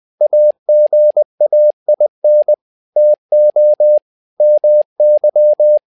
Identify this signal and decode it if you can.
Morse code, heard as a keyed tone, about 20 wpm, 600 Hz. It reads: AGAIN TO MY
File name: learning center / tower trouble / tower trouble 20wpm